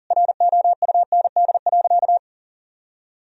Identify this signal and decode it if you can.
Morse code, heard as a keyed tone, about 40 wpm, 700 Hz. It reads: ROUND.
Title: Sentences from Top 1000 Words - Review Part 1 - 40wpm